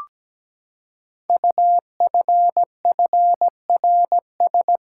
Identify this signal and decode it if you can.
Morse code, heard as a keyed tone, about 17 wpm, 700 Hz. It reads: UFFRS
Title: Code Groups - 5 Letters - 17wpm